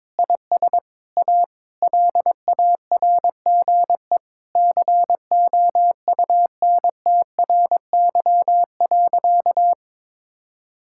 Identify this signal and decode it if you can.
Morse code, heard as a keyed tone, about 22 wpm, 700 Hz. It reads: IS A LARGE COUNTRY.
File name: practice / 22wpm / Sentences with 4-Word Phrases - 22wpm